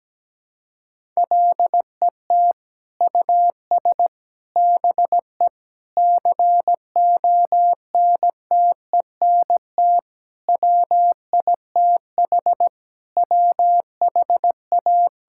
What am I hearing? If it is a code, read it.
Morse code, heard as a keyed tone, about 17 wpm, 700 Hz. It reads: LET US BE CONTENT WITH WHA
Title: Sentences from Top 1000 Words - Review Part 6 - 17wpm